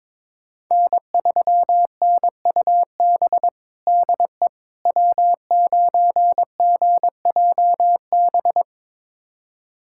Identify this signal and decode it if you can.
Morse code, heard as a keyed tone, about 22 wpm, 700 Hz. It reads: N3NUB DE W9GJB
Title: USA Callsign DE Callsign - 5 Characters - 22wpm